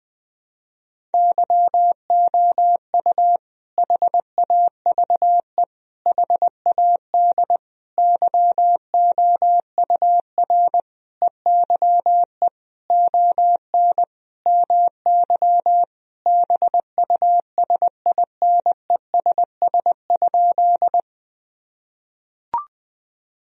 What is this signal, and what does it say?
Morse code, heard as a keyed tone, about 20 wpm, 700 Hz. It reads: YOU HAVE HAD YOUR EYE ON MY BUSINESS?
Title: Sentences from Top 1000 Words - Review Part 1 - 20wpm